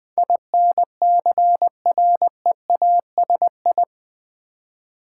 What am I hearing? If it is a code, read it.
Morse code, heard as a keyed tone, about 20 wpm, 700 Hz. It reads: INCREASI
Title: Build A Word - 4 Letters At A Time - 12-Letter Words - 20wpm